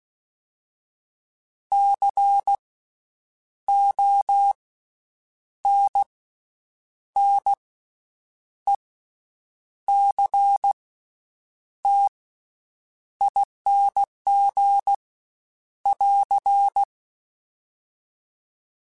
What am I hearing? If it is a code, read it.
Morse code, heard as a keyed tone, about 16 wpm, 800 Hz. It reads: C O N N E C T ING <AR>
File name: Special INGs-mp3s / Special INGs-mp3s / Slow ing @15